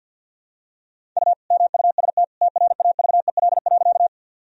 Morse code, heard as a keyed tone, about 51 wpm, 700 Hz. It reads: U MUST TRAVEL.
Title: Sentences from Top 400 Words and Abbreviations 50wpm